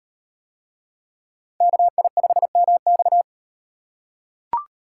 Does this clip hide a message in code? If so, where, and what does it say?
Yes: Morse code from 1.6 to 3.2 s, heard as a keyed tone, about 38 wpm, 700 Hz. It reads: KI5MX